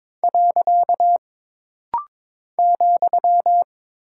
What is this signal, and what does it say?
Morse code, heard as a keyed tone, about 22 wpm, 700 Hz. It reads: . ,
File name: practice / 22wpm / Single Punctuation-Prosign - Rapid-Fire - 22wpm